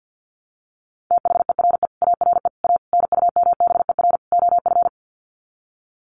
Morse code, heard as a keyed tone, about 50 wpm, 700 Hz. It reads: THERE ARE A NUMBER OF
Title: Sets of 5 Words 50wpm